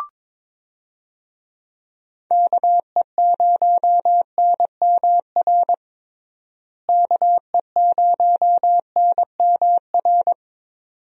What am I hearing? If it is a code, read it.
Morse code, heard as a keyed tone, about 22 wpm, 700 Hz. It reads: KE0NMR KE0NMR
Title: USA Call Signs - 6 Characters - Repeated 2x - 22wpm